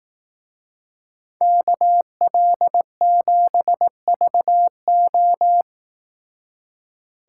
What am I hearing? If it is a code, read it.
Morse code, heard as a keyed tone, about 18 wpm, 700 Hz. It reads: KL7VO
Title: Speed Racing - USA 2x2 Call Signs 15wpm